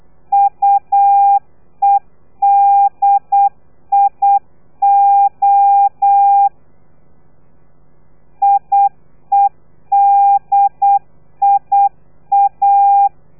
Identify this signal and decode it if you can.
Morse code, heard as a keyed tone, about 8 wpm, 800 Hz. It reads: UEDIO IEDIA
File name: telegrafia / i2tfj / Lez3 35